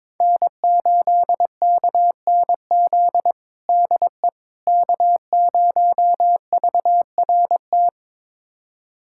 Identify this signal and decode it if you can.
Morse code, heard as a keyed tone, about 21 wpm, 700 Hz. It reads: N8KNZ DE K0VRT
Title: USA Callsign DE Callsign - 5 Characters - 22wpm